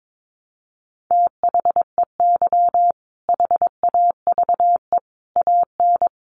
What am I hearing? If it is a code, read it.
Morse code, heard as a keyed tone, about 22 wpm, 700 Hz. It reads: THEY HAVE AN
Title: Sentences from Top 1000 Words 22wpm